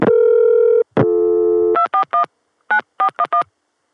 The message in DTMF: 241 9411